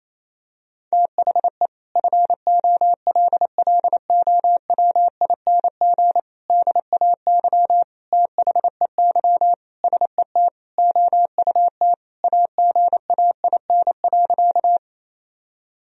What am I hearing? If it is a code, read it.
Morse code, heard as a keyed tone, about 28 wpm, 700 Hz. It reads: THE FOLLOWING DAY THEY SET OUT AGAIN.
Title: Sentences from Top 1000 Words - Review Part 4 - 28wpm